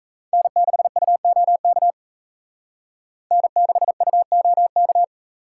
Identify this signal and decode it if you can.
Morse code, heard as a keyed tone, about 34 wpm, 700 Hz. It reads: N6UOK N6UOK
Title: Speed Racing - USA 1x3 Call Signs 28wpm